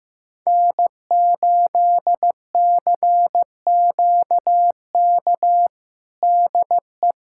Morse code, heard as a keyed tone, about 15 wpm, 700 Hz. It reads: N8CQK DE
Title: USA Callsign DE Callsign - 5 Characters - 15wpm